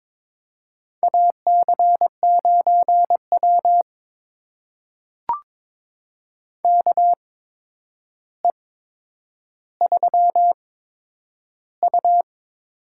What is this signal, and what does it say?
Morse code, heard as a keyed tone, about 22 wpm, spaced as for 9 wpm, 700 Hz. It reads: AC9W KE3U
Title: USA Call Signs - 4 Characters - 8x Character Spacing - 22wpm